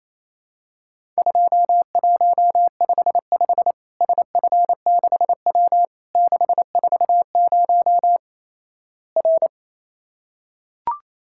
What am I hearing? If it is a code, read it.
Morse code, heard as a keyed tone, about 28 wpm, 700 Hz. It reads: 2155 SF6W 640 R
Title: Worked All Europe DX Contest - QTC Exchange - 28wpm